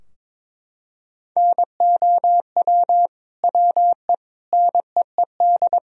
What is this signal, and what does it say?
Morse code, heard as a keyed tone, about 22 wpm, 700 Hz. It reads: NOW WE NEED
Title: Sentences from Top 1000 Words - Review Part 4 - 22wpm